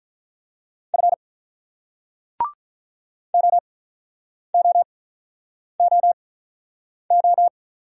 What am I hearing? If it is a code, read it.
Morse code, heard as a keyed tone, about 49 wpm, spaced as for 9 wpm, 700 Hz. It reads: F OOOO